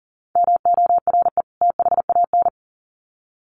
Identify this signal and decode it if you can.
Morse code, heard as a keyed tone, about 40 wpm, 700 Hz. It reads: MORE THAN
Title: Sets of 2 Words 40wpm